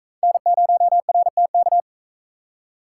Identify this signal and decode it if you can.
Morse code, heard as a keyed tone, about 44 wpm, 700 Hz. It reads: N0RTK